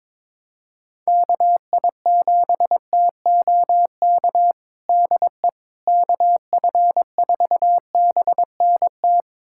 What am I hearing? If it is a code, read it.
Morse code, heard as a keyed tone, about 21 wpm, 700 Hz. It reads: KI7TOK DE KF4BNT
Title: USA Callsign DE Callsign - 6 Characters - 22wpm